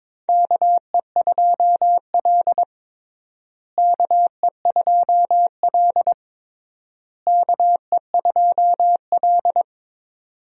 Morse code, heard as a keyed tone, about 22 wpm, 700 Hz. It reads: KE2L KE2L KE2L
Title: USA Call Signs - 4 Characters - Repeated 3x - 22wpm